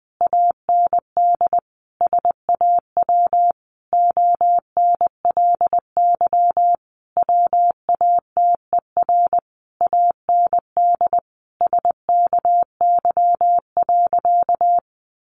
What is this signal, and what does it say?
Morse code, heard as a keyed tone, about 20 wpm, 700 Hz. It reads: AND SAW ONLY WATER AND SKY.
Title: Sentences from Top 1200 Words 20wpm